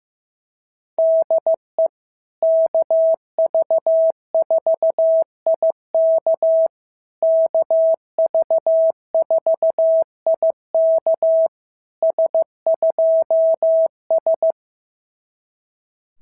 Morse code, heard as a keyed tone, about 15 wpm, 690 Hz. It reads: DE KV4IK KV4IK S2S
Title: SOTA QSO Practice 15wpm